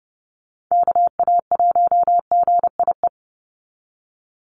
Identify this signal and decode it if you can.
Morse code, heard as a keyed tone, about 28 wpm, 700 Hz. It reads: KA1GIE
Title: USA Call Signs 30wpm